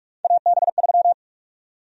Morse code, heard as a keyed tone, about 46 wpm, 700 Hz. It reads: AB3
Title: USA Call Sign Prefixes - 45wpm